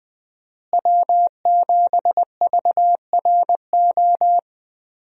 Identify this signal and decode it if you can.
Morse code, heard as a keyed tone, about 20 wpm, 700 Hz. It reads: W7VRO
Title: USA Call Signs - 5 Characters - Repeated 3x - 20wpm